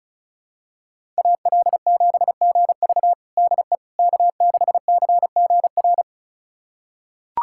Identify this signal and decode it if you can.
Morse code, heard as a keyed tone, about 36 wpm, 700 Hz. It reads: AL7GV DE K6CGR
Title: USA Callsign DE Callsign - 5 Characters - 35wpm